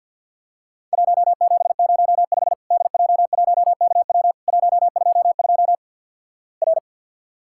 Morse code, heard as a keyed tone, about 50 wpm, 700 Hz. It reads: 1705 DJ1KW 122 R
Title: Worked All Europe DX Contest - QTC Exchange with Repeats - 50wpm